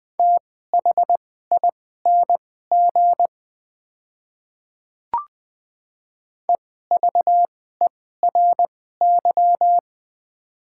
Morse code, heard as a keyed tone, about 20 wpm, spaced as for 14 wpm, 700 Hz. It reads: THING EVERY
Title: 2x Character Spacing - Top 200 Words - 20wpm